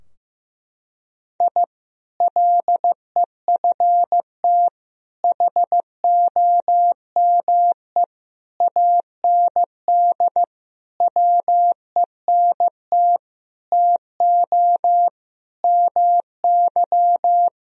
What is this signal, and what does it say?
Morse code, heard as a keyed tone, about 15 wpm, 700 Hz. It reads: I LEFT HOME AND WENT TO MY
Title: Sentences from Top 1000 Words - Review Part 3 - 15wpm